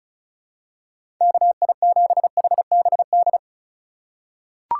Morse code, heard as a keyed tone, about 36 wpm, 700 Hz. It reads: KI7HBD
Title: USA Call Signs - 6 Characters - 3x Character Spacing - 35wpm